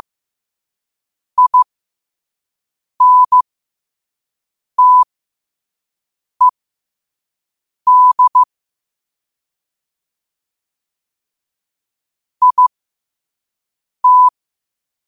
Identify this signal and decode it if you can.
Morse code, heard as a keyed tone, about 15 wpm, spaced as for 5 wpm, 1000 Hz. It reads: INTED IT
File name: qotdfiles / 5wpm